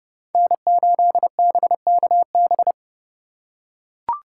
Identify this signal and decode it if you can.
Morse code, heard as a keyed tone, about 29 wpm, 700 Hz. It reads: N8BKB
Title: USA Call Signs - 5 Characters - 3x Character Spacing - 30wpm